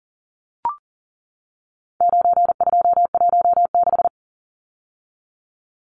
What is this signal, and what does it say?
Morse code, heard as a keyed tone, about 40 wpm, 700 Hz. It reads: 9216
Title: Numbers - 4-Digit 40wpm